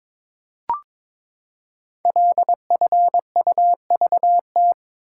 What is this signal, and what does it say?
Morse code, heard as a keyed tone, about 22 wpm, 700 Hz. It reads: LFUVT
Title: Code Groups - 5 Letters - 22wpm